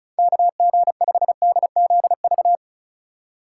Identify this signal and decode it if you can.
Morse code, heard as a keyed tone, about 36 wpm, 700 Hz. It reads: KG5DZV
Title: USA Call Signs - 6 Characters - 2x Character Spacing - 35wpm